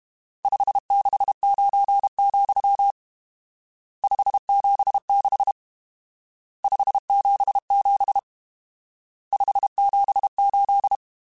Morse code, heard as a keyed tone, about 32 wpm, 800 Hz. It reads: 569, 576 577 578
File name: RSTs / RST-mp3s / Various RST @30